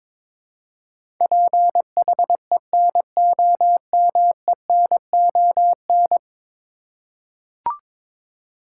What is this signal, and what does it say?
Morse code, heard as a keyed tone, about 21 wpm, 700 Hz. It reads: PHENOMENON
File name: practice / 22wpm / 10 Letter Words - First 3 Letters Given - 22wpm